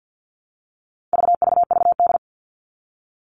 Morse code, heard as a keyed tone, about 50 wpm, 700 Hz. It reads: VVVD
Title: All Cut Numbers - 4-Digit 50wpm